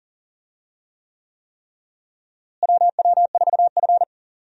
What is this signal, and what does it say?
Morse code, heard as a keyed tone, about 37 wpm, 700 Hz. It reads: WW4F
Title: USA Call Signs - 4 Characters - Repeated 3x - 40wpm